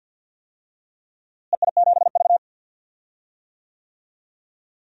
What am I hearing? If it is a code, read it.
Morse code, heard as a keyed tone, about 47 wpm, 700 Hz. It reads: EI7V
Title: DXCC-100 Call Signs - 4 Characters - Repeated 3x - 50wpm